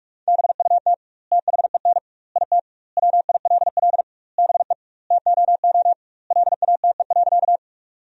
Morse code, heard as a keyed tone, about 43 wpm, 700 Hz. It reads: BUT THEN IT WILL BE TOO LATE.
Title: 2x Word-Spacing - Sentences from Top 300 Words - 45wpm